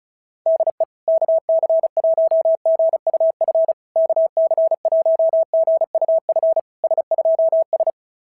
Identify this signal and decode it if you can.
Morse code, heard as a keyed tone, about 33 wpm, 650 Hz. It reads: DE KC1GUF KC1GUF S2S
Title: SOTA QSO Practice 35wpm